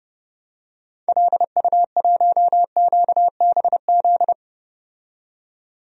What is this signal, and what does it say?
Morse code, heard as a keyed tone, about 30 wpm, 700 Hz. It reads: LU1QBZ
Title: DXCC-100 Call Signs - 6 Characters - 30wpm